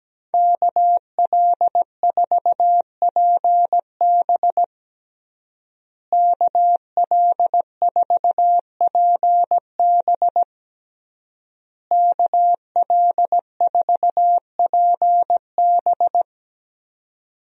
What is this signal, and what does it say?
Morse code, heard as a keyed tone, about 17 wpm, 700 Hz. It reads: KL4PB KL4PB KL4PB
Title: USA Call Signs - 5 Characters - Repeated 3x - 17wpm